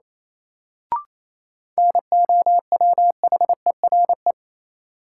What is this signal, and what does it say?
Morse code, heard as a keyed tone, about 28 wpm, 700 Hz. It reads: NOWHERE
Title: Sets of 5 Words Encore - New Word Drill - 28wpm